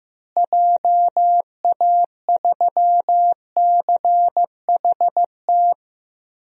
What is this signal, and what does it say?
Morse code, heard as a keyed tone, about 15 wpm, 700 Hz. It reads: JA3CHT
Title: DXCC-100 Call Signs - 6 Characters - Repeated 3x - 15wpm